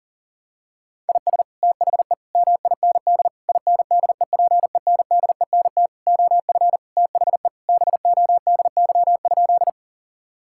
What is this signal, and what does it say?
Morse code, heard as a keyed tone, about 40 wpm, 700 Hz. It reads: IS THE MIND INDEPENDENT OF THE BODY?
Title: Sentences from Top 1000 Words - Review Part 6 - 40wpm